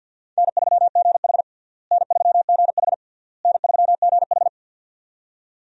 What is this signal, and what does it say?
Morse code, heard as a keyed tone, about 47 wpm, 700 Hz. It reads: N3GH N3GH N3GH
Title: USA Call Signs - 4 Characters - Repeated 3x - 50wpm